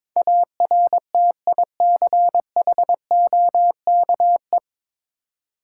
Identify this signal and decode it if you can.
Morse code, heard as a keyed tone, about 23 wpm, 690 Hz. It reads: ARTICHOKE
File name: Difficult Words with Hints - 22wpm